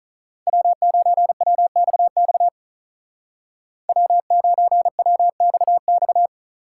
Morse code, heard as a keyed tone, about 44 wpm, 700 Hz. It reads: W9WXX W9WXX
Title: Speed Racing - USA 1x3 Call Signs 35wpm